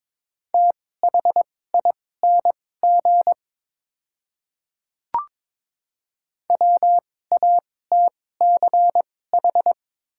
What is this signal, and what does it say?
Morse code, heard as a keyed tone, about 22 wpm, spaced as for 16 wpm, 700 Hz. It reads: THING WATCH
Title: 2x Character Spacing - Top 300 Words - 22wpm